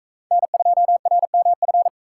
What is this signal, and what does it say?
Morse code, heard as a keyed tone, about 43 wpm, 700 Hz. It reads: N2RMF